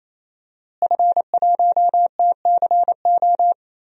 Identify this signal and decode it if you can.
Morse code, heard as a keyed tone, about 28 wpm, 700 Hz. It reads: F1TCO
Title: DXCC-100 Call Signs - 5 Characters - Repeated 3x - 28wpm